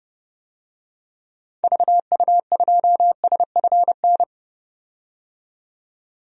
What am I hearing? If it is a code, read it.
Morse code, heard as a keyed tone, about 30 wpm, 700 Hz. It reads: VU2SFN
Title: DXCC-100 Call Signs - 6 Characters - 30wpm